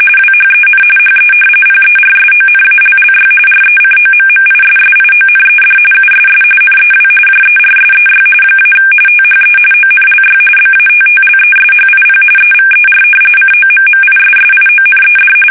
RTTY, about 45.45 baud. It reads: WHEN HE PERSONALLY RECOMMISSIONED USS NEW JERSEY - THE ONLY BATTLESHIP EVER COMMISSIONED
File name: fsk850 / battleshipnj